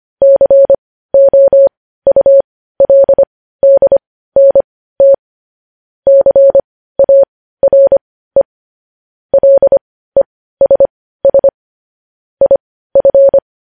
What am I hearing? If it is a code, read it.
Morse code, heard as a keyed tone, about 25 wpm, spaced as for 15 wpm, 550 Hz. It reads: COULDNT CARE LESS IF
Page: news-headlines-in-morse-code-at-15-wpm